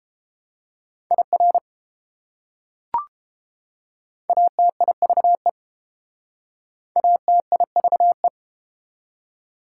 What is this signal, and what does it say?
Morse code, heard as a keyed tone, about 30 wpm, 700 Hz. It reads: IR ATIVE ATIVE